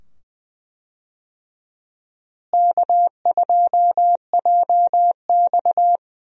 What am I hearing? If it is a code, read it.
Morse code, heard as a keyed tone, about 20 wpm, 700 Hz. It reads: K2JX
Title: USA Call Signs - 4 Characters - Repeated 2x - 20wpm